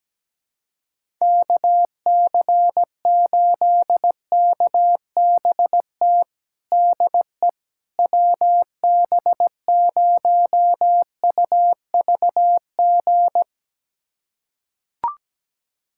Morse code, heard as a keyed tone, about 17 wpm, 700 Hz. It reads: KC8KBT DE WB0UVG
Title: USA Callsign DE Callsign - 6 Characters - 17wpm